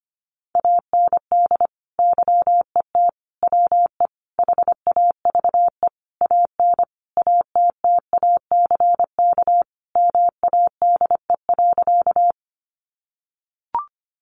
Morse code, heard as a keyed tone, about 25 wpm, 700 Hz. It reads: AND YET WE HAVE AN ATTACK MADE.
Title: Sentences from Top 1500 Words 25wpm